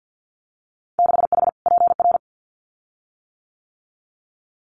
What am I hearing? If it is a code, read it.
Morse code, heard as a keyed tone, about 50 wpm, 700 Hz. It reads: 6H PR
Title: Winter Field Day Exchanges Encore 50wpm